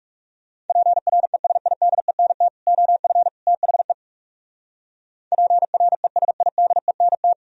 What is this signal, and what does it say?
Morse code, heard as a keyed tone, about 37 wpm, 700 Hz. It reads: PRESIDENT OF THE PRESIDENT
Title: Speed-Racing - Sets of 3 Words 30wpm